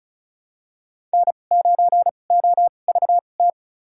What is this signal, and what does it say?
Morse code, heard as a keyed tone, about 36 wpm, spaced as for 25 wpm, 700 Hz. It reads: N9OVT
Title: USA Call Signs - 5 Characters - 2x Character Spacing - 35wpm